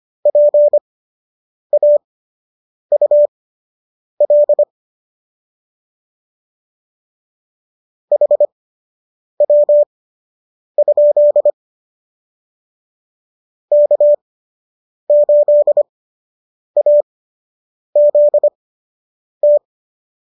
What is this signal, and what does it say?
Morse code, heard as a keyed tone, about 25 wpm, spaced as for 8 wpm, 590 Hz. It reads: PAUL HW? K8AZT
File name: qso006 08